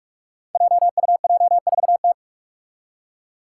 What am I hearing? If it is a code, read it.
Morse code, heard as a keyed tone, about 45 wpm, 700 Hz. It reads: JUJ4T
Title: Five Challenging Letter-Number Combinations 45wpm